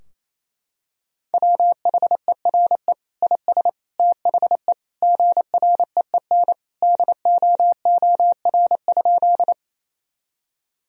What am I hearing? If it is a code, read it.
Morse code, heard as a keyed tone, about 28 wpm, 700 Hz. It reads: WHERE IS THE GREEN DOOR?
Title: Sentences from Top 1000 Words - Review Part 2 - 28wpm